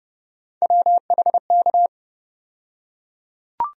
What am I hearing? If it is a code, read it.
Morse code, heard as a keyed tone, about 30 wpm, 700 Hz. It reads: WHK
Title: USA Call Sign Postfixes - 3 Letters 30wpm